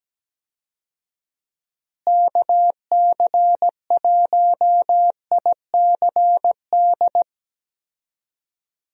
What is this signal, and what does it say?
Morse code, heard as a keyed tone, about 17 wpm, 700 Hz. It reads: KC1ICD
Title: USA Call Signs - 6 Characters - Repeated 2x - 17wpm